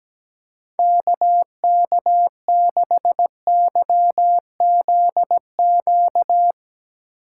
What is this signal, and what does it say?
Morse code, heard as a keyed tone, about 17 wpm, 700 Hz. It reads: KK6YZQ
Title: Speed Racing - USA 2x3 Call Signs 17wpm